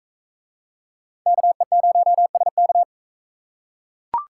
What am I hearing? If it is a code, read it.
Morse code, heard as a keyed tone, about 42 wpm, 700 Hz. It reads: KE0SK